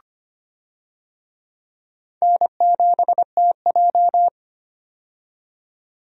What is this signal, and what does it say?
Morse code, heard as a keyed tone, about 25 wpm, 700 Hz. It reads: N7TJ
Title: USA Call Signs - 4 Characters - 25wpm